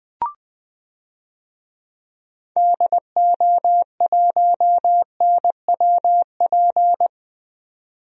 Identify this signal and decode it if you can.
Morse code, heard as a keyed tone, about 21 wpm, 700 Hz. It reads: DO1NWP
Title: DXCC-100 Call Signs - 6 Characters - 20wpm